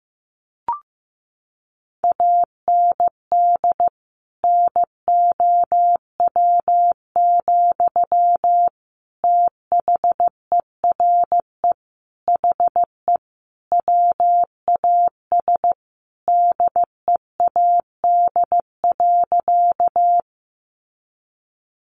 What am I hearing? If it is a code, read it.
Morse code, heard as a keyed tone, about 14 wpm, 700 Hz. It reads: AND NOW, THERE HE WAS DEAD.
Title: Sentences from Top 1500 Words 15wpm